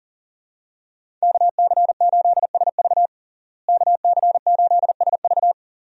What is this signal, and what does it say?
Morse code, heard as a keyed tone, about 40 wpm, 700 Hz. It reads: KC8SV KC8SV
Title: USA Call Signs - 5 Characters - Repeated 3x - 40wpm